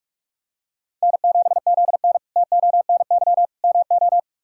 Morse code, heard as a keyed tone, about 45 wpm, 700 Hz. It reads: N7ZN TONY MO
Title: K1USN SST QSO Practice - 45wpm